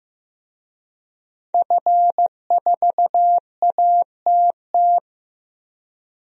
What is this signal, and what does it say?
Morse code, heard as a keyed tone, about 16 wpm, 700 Hz. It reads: F4ATT
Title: DXCC-100 Call Signs - 5 Characters - Repeated 3x - 15wpm